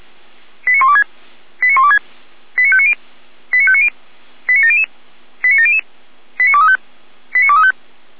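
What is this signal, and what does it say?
Five-tone selective call (ZVEI), heard as quick runs of five tones, tones of about 70 ms. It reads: 89126 89126 89590 89590 89700 89700 89235 89235
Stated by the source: ZVEI five-tone selective call received on SDR, fire brigade feuerwehr-orlamuende.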